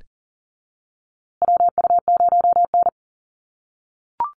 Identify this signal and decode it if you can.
Morse code, heard as a keyed tone, about 40 wpm, 700 Hz. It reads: WU0N
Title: USA Call Signs Repeated 2x 40wpm